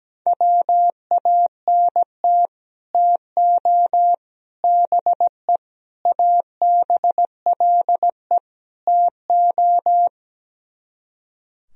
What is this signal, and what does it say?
Morse code, heard as a keyed tone, about 17 wpm, 700 Hz. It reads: WANT TO BE ABLE TO
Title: Speed-Racing - Sets of 5 Words 17wpm